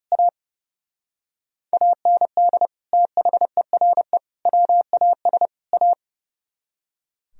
Morse code, heard as a keyed tone, about 40 wpm, 700 Hz. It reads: A AND THERE WAS A